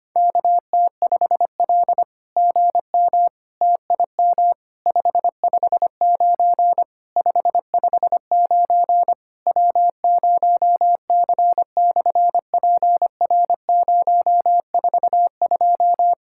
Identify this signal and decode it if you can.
Morse code, heard as a keyed tone, about 24 wpm, 700 Hz. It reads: KT5L GM TIM 559 559 W0C/PR042
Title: SOTA QSO Practice 25wpm